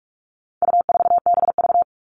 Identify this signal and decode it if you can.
Morse code, heard as a keyed tone, about 44 wpm, 700 Hz. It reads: U4BV
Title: All Cut Numbers - 4-Digit 45wpm